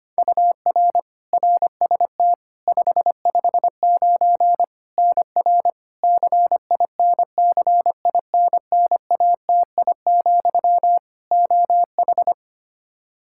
Morse code, heard as a keyed tone, about 25 wpm, 700 Hz. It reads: UR RST 559 NR CINCINNATI, OH
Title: K1USN SST QSO Practice 25wpm